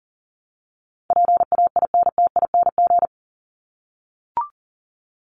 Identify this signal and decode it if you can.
Morse code, heard as a keyed tone, about 40 wpm, 700 Hz. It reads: PAINTING
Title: ING Suffix - Top 100 Words with Base Words 40wpm